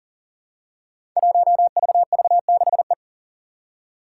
Morse code, heard as a keyed tone, about 42 wpm, 690 Hz. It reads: 1VV6E
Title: Five Challenging Letter-Number Combinations 40wpm